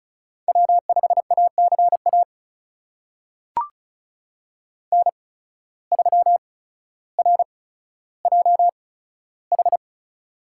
Morse code, heard as a keyed tone, about 36 wpm, spaced as for 15 wpm, 700 Hz. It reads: W5ACA N3RJH
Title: USA Call Signs - 8x Character Spacing - 35wpm